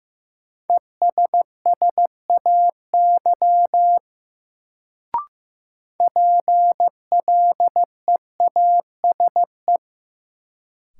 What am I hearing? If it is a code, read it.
Morse code, heard as a keyed tone, about 15 wpm, 700 Hz. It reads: ESSAY PLEASE
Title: Sentences from Top 1500 Words - New Word Drill - 15wpm